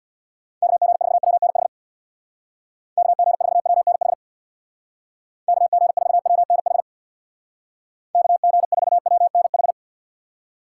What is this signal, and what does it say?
Morse code, heard as a keyed tone, about 60 wpm, 700 Hz. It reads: KG4WNH KG4WNH KG4WNH KG4WNH
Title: Speed Racing - USA 2x3 Call Signs 50wpm